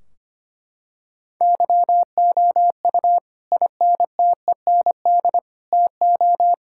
Morse code, heard as a keyed tone, about 25 wpm, 700 Hz. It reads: YOU INTEND TO
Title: Sentences from Top 1600 Words 25wpm